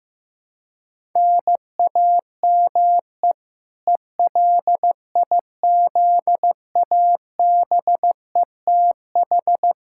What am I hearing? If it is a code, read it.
Morse code, heard as a keyed tone, about 15 wpm, 700 Hz. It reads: NAME ELIZABETH
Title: K1USN SST QSO Practice 15wpm